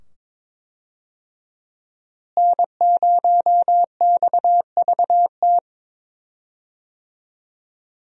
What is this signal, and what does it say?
Morse code, heard as a keyed tone, about 22 wpm, 700 Hz. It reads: N0XVT
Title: USA Call Signs - 5 Characters - Repeated 2x - 22wpm